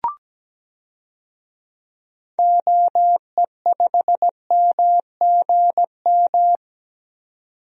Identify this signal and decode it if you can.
Morse code, heard as a keyed tone, about 17 wpm, 700 Hz. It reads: OE5MGM
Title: DXCC-100 Call Signs - 6 Characters - 17wpm